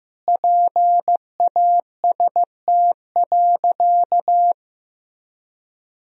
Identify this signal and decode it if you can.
Morse code, heard as a keyed tone, about 15 wpm, 700 Hz. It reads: PAST.
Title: Sentences from Top 1000 Words - Review Part 3 - 15wpm